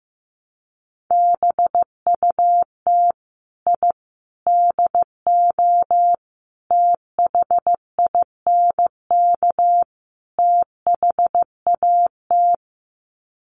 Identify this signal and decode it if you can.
Morse code, heard as a keyed tone, about 14 wpm, 700 Hz. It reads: BUT I DO THINK THAT
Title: Sets of 5 Words Encore2 15wpm